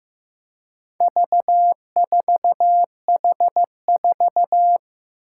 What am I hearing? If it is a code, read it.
Morse code, heard as a keyed tone, about 15 wpm, 700 Hz. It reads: V4H4
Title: Four Challenging Letter-Number Combinations 15wpm